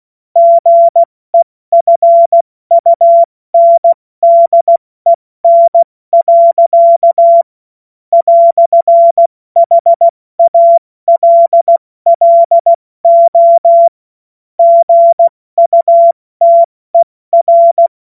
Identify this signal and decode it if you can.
Morse code, heard as a keyed tone, about 16 wpm, 680 Hz. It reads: GEFUNDEN. "HALLO GUTER
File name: Morsekurs / Witze / Witz 03 Juni Mai 2025 70 F80